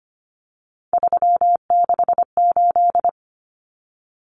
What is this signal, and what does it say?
Morse code, heard as a keyed tone, about 25 wpm, 700 Hz. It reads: 368
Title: Numbers - 3-Digit 25wpm